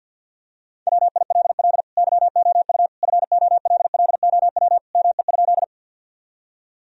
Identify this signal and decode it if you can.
Morse code, heard as a keyed tone, about 51 wpm, 700 Hz. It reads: WILL YOU FOLLOW ME?
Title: Sentences from Top 1000 Words - Review Part 1 - 50wpm